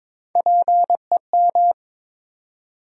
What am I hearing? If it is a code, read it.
Morse code, heard as a keyed tone, about 22 wpm, 700 Hz. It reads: PEM
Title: USA Call Sign Postfixes - 3 Letters 22wpm